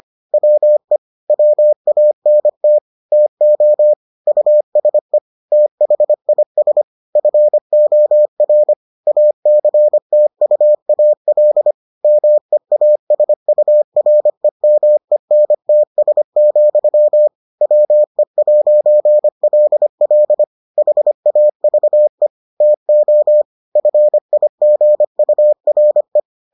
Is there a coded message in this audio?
Yes: Morse code from 0.3 to 26.2 s, heard as a keyed tone, about 25 wpm, 600 Hz. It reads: WE WANT TO USE THIS FOR ACTUAL MEASUREMENTS, WE'LL HAVE TO FIGURE